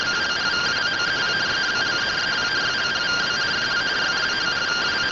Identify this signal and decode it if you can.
Two-tone teleprinter signal, about 170 Hz shift.